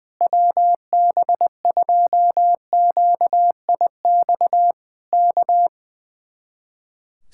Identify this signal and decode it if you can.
Morse code, heard as a keyed tone, about 19 wpm, 700 Hz. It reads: WB2QIX K
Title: Speed-Racing Common QSO Phrases - 20wpm